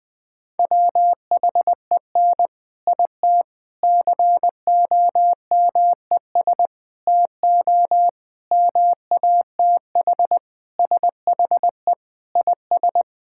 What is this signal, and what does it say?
Morse code, heard as a keyed tone, about 20 wpm, 700 Hz. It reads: WHEN IT COMES TO MATH SHE IS
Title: Sentences with 3-Word Phrases - 20wpm